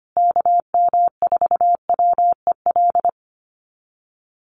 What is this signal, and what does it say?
Morse code, heard as a keyed tone, about 25 wpm, 700 Hz. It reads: KM4WEL
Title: USA Call Signs Repeated 2x 25wpm